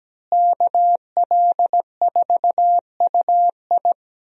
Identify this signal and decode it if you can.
Morse code, heard as a keyed tone, about 17 wpm, 700 Hz. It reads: KL4UI
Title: USA Call Signs - 5 Characters - Repeated 3x - 17wpm